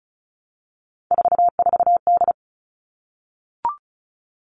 Morse code, heard as a keyed tone, about 34 wpm, 700 Hz. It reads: 44D
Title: All Cut Numbers - 3-Digit 35wpm